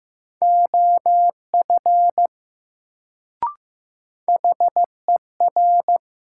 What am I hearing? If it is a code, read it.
Morse code, heard as a keyed tone, about 15 wpm, 700 Hz. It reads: OF HER
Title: Sentences from Top 100 Words - New Word Drill - 15wpm